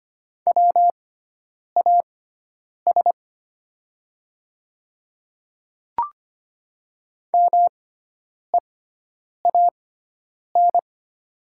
Morse code, heard as a keyed tone, about 25 wpm, spaced as for 9 wpm, 690 Hz. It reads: WAS MEAN
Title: 6x Character Spacing - Top 200 Words - 25wpm